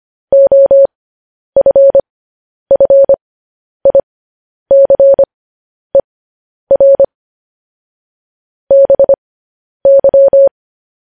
Morse code, heard as a keyed tone, about 25 wpm, spaced as for 10 wpm, 550 Hz. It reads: OFFICER BY
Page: news-headlines-in-morse-code-at-10-wpm